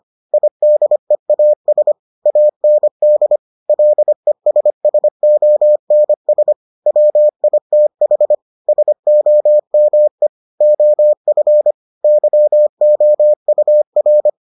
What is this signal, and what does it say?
Morse code, heard as a keyed tone, about 25 wpm, 600 Hz. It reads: IDEAS AND LESSONS WITH SOME OF YOUR